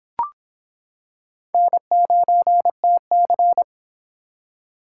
Morse code, heard as a keyed tone, about 25 wpm, 700 Hz. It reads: N9TC